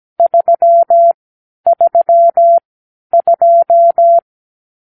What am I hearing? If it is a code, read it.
Morse code, heard as a keyed tone, about 17 wpm, spaced as for 10 wpm, 680 Hz. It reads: 332
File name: k7qo / Lesson 44 (1-3)